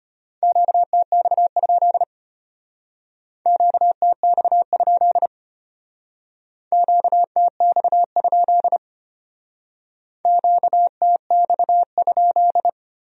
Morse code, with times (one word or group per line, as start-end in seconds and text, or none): 0.4-2.0 QTX?
3.5-5.3 QTX?
6.7-8.8 QTX?
10.3-12.7 QTX?